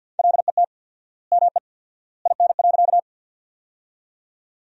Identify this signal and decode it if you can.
Morse code, heard as a keyed tone, about 50 wpm, 700 Hz. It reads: LET ME IN.
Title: 4x Word-Spacing - Sentences from Top 300 Words - 50wpm